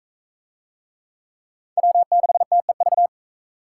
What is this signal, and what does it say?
Morse code, heard as a keyed tone, about 41 wpm, 700 Hz. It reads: W6TEV